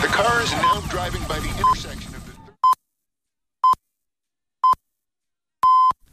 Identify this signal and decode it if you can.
Radio time signal (time pips), with speech. It is heard from 0.6 to 5.9 s.